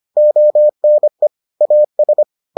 Morse code, heard as a keyed tone, about 25 wpm, 600 Hz. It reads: ONE AS